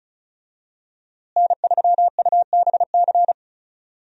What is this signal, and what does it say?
Morse code, heard as a keyed tone, about 34 wpm, 700 Hz. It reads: N3UBC
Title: USA Call Signs - 5 Characters - Repeated 3x - 35wpm